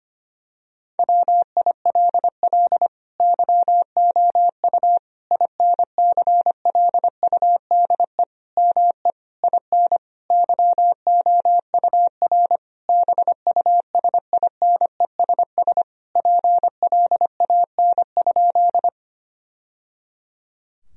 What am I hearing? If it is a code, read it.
Morse code, heard as a keyed tone, about 25 wpm, 700 Hz. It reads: WILL YOU INCLUDE ME IN YOUR BUSINESS PLAN?
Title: Sentences from Top 1000 Words - Review Part 3 - 25wpm